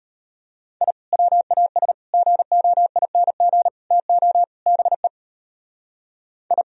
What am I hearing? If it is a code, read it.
Morse code, heard as a keyed tone, about 32 wpm, 700 Hz. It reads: I WAS GOING TO BE I